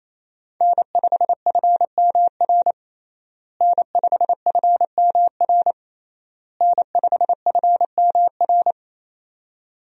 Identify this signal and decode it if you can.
Morse code, heard as a keyed tone, about 28 wpm, 700 Hz. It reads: N5FMR N5FMR N5FMR
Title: USA Call Signs - 5 Characters - Repeated 3x - 28wpm